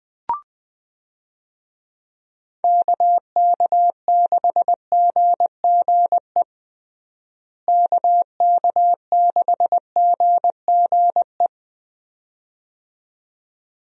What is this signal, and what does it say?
Morse code, heard as a keyed tone, about 20 wpm, 700 Hz. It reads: KK6GGE KK6GGE
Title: USA Call Signs - 6 Characters - Repeated 2x - 20wpm